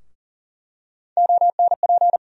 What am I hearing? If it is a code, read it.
Morse code, heard as a keyed tone, about 40 wpm, 700 Hz. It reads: ONP
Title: USA Call Sign Postfixes - 3 Letters 40wpm